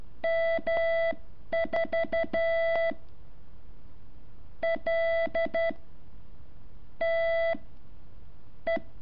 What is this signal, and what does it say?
Morse code, heard as a keyed tone, about 12 wpm, spaced as for 11 wpm, 660 Hz. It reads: M4 L T E
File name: curso / ejercicio 8